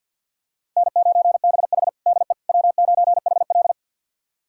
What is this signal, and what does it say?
Morse code, heard as a keyed tone, about 51 wpm, 700 Hz. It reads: N9BH DE W9HL
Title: USA Callsign DE Callsign - 4 Characters - 50wpm